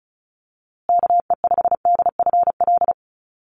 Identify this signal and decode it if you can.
Morse code, heard as a keyed tone, about 34 wpm, 700 Hz. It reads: KE5DFL
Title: USA Call Signs Repeated 3x 35wpm